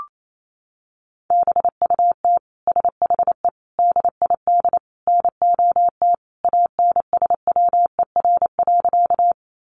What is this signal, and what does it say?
Morse code, heard as a keyed tone, about 28 wpm, 700 Hz. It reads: BUT SHE DID NOT ANSWER.
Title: Sentences from Top 600 Words 28wpm